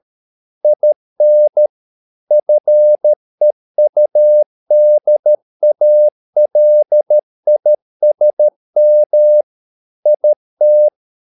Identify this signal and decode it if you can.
Morse code, heard as a keyed tone, about 13 wpm, 600 Hz. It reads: IN FEUDALISM IT
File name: mp3 files / Lexiphiles - 13wpm